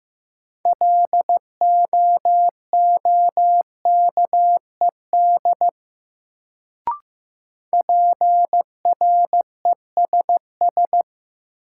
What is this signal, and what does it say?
Morse code, heard as a keyed tone, about 15 wpm, 700 Hz. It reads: LOOKED PRESS
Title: Sentences from Top 800 Words - New Word Drill - 15wpm